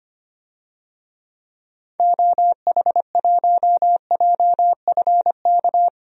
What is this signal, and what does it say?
Morse code, heard as a keyed tone, about 25 wpm, 700 Hz. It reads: OH1JFK
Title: DXCC-100 Call Signs - 6 Characters - 25wpm